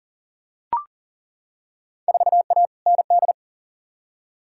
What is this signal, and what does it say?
Morse code, heard as a keyed tone, about 40 wpm, 700 Hz. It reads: 4A ND
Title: Field Day Exchanges Encore - 40wpm